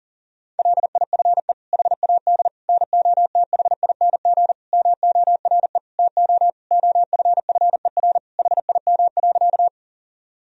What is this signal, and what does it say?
Morse code, heard as a keyed tone, about 37 wpm, 700 Hz. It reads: LIFE HAD NOTHING MORE TO OFFER HIM.
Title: Sentences from Top 1000 Words - Review Part 4 - 40wpm